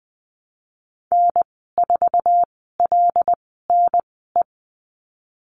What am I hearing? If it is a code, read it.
Morse code, heard as a keyed tone, about 19 wpm, spaced as for 14 wpm, 700 Hz. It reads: N4LNE
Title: 179 Lesson 29g - Slash - Call Signs 20wpm